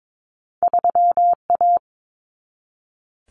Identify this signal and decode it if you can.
Morse code, heard as a keyed tone, about 22 wpm, 700 Hz. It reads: 3A
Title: Common Cut Numbers - 2-Digit 22wpm